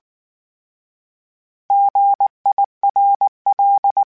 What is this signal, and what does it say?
Morse code, heard as a keyed tone, about 19 wpm, 800 Hz. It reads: GIRL